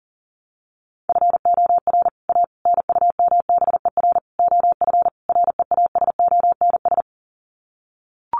Morse code, heard as a keyed tone, about 40 wpm, 700 Hz. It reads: FOR A NUMBER OF REASONS
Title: Sets of 5 Words Encore2 40wpm